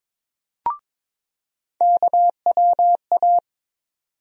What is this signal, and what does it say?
Morse code, heard as a keyed tone, about 22 wpm, 700 Hz. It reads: KWA
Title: USA Call Sign Postfixes - 3 Letters 22wpm